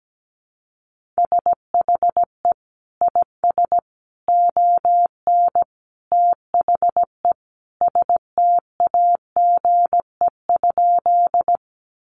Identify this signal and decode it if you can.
Morse code, heard as a keyed tone, about 17 wpm, 700 Hz. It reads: SHE IS ON THE STAGE?
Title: Sentences from Top 1200 Words 17wpm